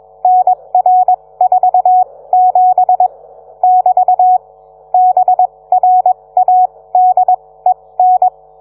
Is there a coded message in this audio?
Yes: Morse code from 0.2 to 8.3 s, heard as a keyed tone, about 21 wpm, spaced as for 17 wpm, 740 Hz. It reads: NR47 <BT> BRADEN